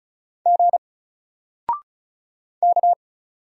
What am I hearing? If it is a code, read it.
Morse code, heard as a keyed tone, about 36 wpm, 700 Hz. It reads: G K
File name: Single-Letter-Number-35wpm